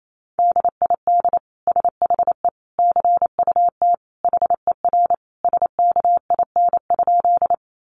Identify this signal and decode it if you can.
Morse code, heard as a keyed tone, about 28 wpm, 700 Hz. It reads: DID SHE CUT HER SKIN?
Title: Sentences from Top 1300 Words 28wpm